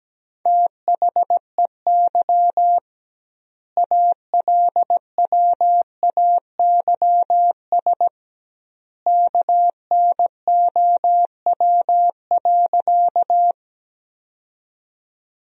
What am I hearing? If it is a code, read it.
Morse code, heard as a keyed tone, about 17 wpm, 700 Hz. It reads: THEY ALWAYS KNOW.
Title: 2x Word-Spacing - Sentences from Top 300 Words - 17wpm